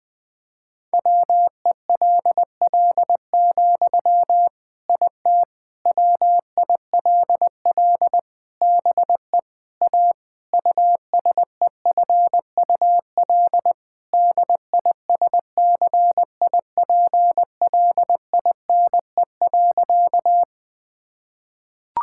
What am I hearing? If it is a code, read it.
Morse code, heard as a keyed tone, about 20 wpm, 700 Hz. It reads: WELL, IT WILL BE A USEFUL DISCIPLINE.
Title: Sentences from Top 1000 Words - Review Part 6 - 20wpm